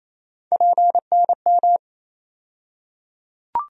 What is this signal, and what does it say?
Morse code, heard as a keyed tone, about 28 wpm, 700 Hz. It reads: PNM
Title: USA Call Sign Postfixes - 3 Letters 28wpm